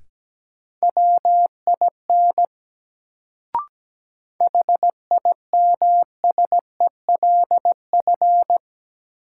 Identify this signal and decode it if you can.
Morse code, heard as a keyed tone, about 17 wpm, 700 Hz. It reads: WIN HIMSELF
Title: Sentences from Top 1500 Words - New Word Drill - 17wpm